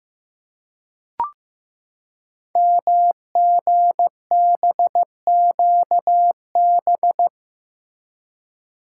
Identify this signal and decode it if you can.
Morse code, heard as a keyed tone, about 15 wpm, 700 Hz. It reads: MGBQB
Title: Code Groups - 5 Letters - 15wpm